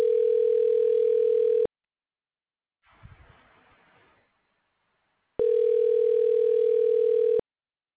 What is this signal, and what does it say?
Ringback tone (North America), 440 + 480 Hz.